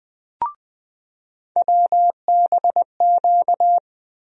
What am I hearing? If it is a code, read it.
Morse code, heard as a keyed tone, about 20 wpm, 700 Hz. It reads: WBQ
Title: USA Call Sign Postfixes - 3 Letters 20wpm